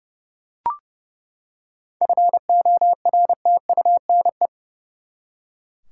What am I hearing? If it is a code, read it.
Morse code, heard as a keyed tone, about 29 wpm, 700 Hz. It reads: FORTUNE
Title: Seven Letter Words - 30wpm